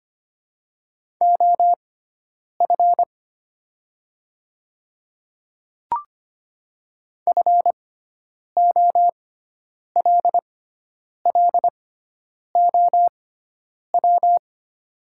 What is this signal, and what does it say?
Morse code, heard as a keyed tone, about 25 wpm, spaced as for 9 wpm, 700 Hz. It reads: OF FOLLOW
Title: 6x Character Spacing - Top 300 Words - 25wpm